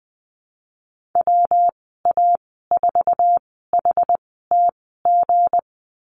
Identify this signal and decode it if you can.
Morse code, heard as a keyed tone, about 20 wpm, spaced as for 14 wpm, 700 Hz. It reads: WA4HTG
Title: 161 Lesson 26g - Letter g - Call Signs 20wpm